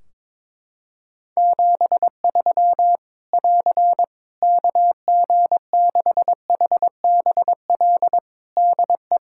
Morse code, heard as a keyed tone, about 22 wpm, 700 Hz. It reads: 73 <AR> KG6HBL DE
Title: Practice QSO - Exchange 4 - Ending - 22wpm